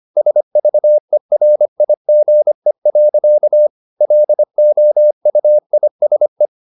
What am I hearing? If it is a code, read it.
Morse code, heard as a keyed tone, about 25 wpm, 600 Hz. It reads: SVERIGE. LOUISE
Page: Ferie 25 wpm